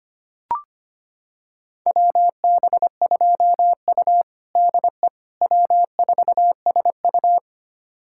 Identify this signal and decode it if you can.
Morse code, heard as a keyed tone, about 25 wpm, 700 Hz. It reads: WB2U DE W4SU
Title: USA Callsign DE Callsign - 4 Characters - 25wpm